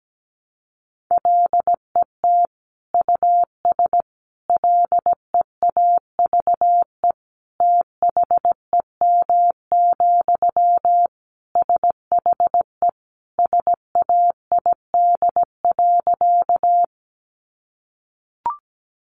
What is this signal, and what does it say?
Morse code, heard as a keyed tone, about 17 wpm, 700 Hz. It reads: LET US LEAVE THEM, SHE SAID.
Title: Sentences from Top 400 Words 17wpm